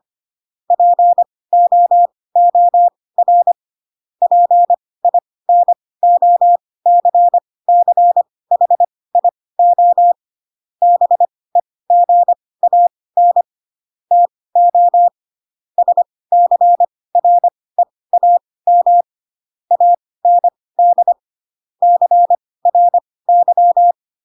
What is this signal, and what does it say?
Morse code, heard as a keyed tone, about 25 wpm, spaced as for 18 wpm, 700 Hz. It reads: POOR PINOCCHIO BEGAN TO SCREAM AND CRY
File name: Pinocchio / Pinocchio ch7 12 25 ik1hge